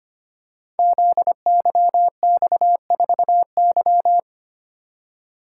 Morse code, heard as a keyed tone, about 25 wpm, 700 Hz. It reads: ZYX4Y
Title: Code Groups - 5 Letter:Numbers - 25wpm